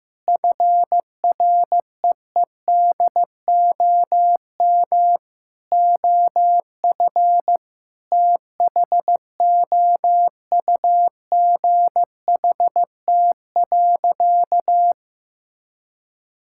Morse code, heard as a keyed tone, about 15 wpm, 700 Hz. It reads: FREEDOM OF THOUGHT.
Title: Sentences from Top 1000 Words - Review Part 5 - 15wpm